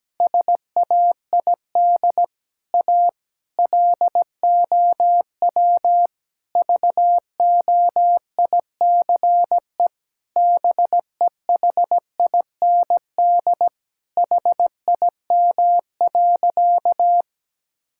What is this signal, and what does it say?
Morse code, heard as a keyed tone, about 17 wpm, 700 Hz. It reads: SAID A LOW VOICE BEHIND HIM.
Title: Sentences from Top 1600 Words 17wpm